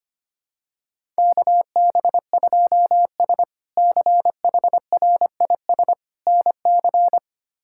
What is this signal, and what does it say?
Morse code, heard as a keyed tone, about 25 wpm, 700 Hz. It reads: KB2S CHRIS NC
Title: K1USN SST QSO Practice - 25wpm